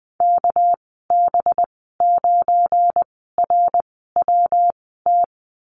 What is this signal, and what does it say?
Morse code, heard as a keyed tone, about 20 wpm, spaced as for 15 wpm, 700 Hz. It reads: KB9RWT
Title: 200 Lesson 32g - Letter k - Call Signs 20wpm